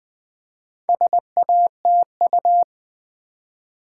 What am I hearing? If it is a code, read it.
Morse code, heard as a keyed tone, about 20 wpm, 700 Hz. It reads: SATU
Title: Build A Word - 2 Letters At A Time - 8-Letter Words - 20wpm